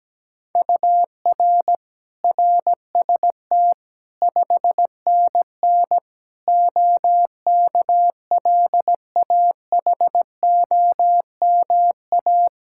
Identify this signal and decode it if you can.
Morse code, heard as a keyed tone, about 17 wpm, 700 Hz. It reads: UR RST 5NN OKLAHOMA
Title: Practice QSO - Exchange 1 - Beginning - 17wpm